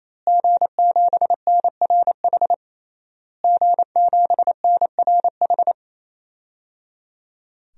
Morse code, heard as a keyed tone, about 28 wpm, 700 Hz. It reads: G7NRH G7NRH
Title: DXCC-100 Call Signs - 5 Characters - Repeated 3x - 28wpm